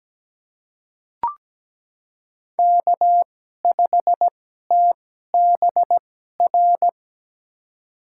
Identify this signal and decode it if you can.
Morse code, heard as a keyed tone, about 17 wpm, spaced as for 12 wpm, 700 Hz. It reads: K5TBR
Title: USA Call Signs - 5 Characters - 2x Character Spacing - 17wpm